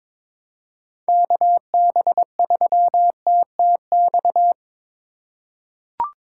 Morse code, heard as a keyed tone, about 21 wpm, 700 Hz. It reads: KB3TTX
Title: USA Call Signs - 6 Characters - 2x Character Spacing - 22wpm